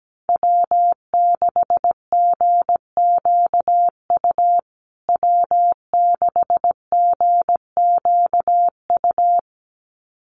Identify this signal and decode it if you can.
Morse code, heard as a keyed tone, about 17 wpm, 700 Hz. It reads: W6GQU W6GQU
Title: USA Call Signs Repeated 2x 17wpm